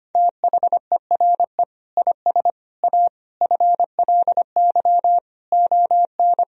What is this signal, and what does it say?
Morse code, heard as a keyed tone, about 25 wpm, 700 Hz. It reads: THERE IS A FLY ON
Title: Sentences from Top 1000 Words - Review Part 2 - 25wpm